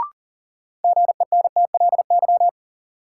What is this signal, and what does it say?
Morse code, heard as a keyed tone, about 40 wpm, 700 Hz. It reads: GENTLY
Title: Sentences from Top 1500 Words - New Word Drill Alt - 40wpm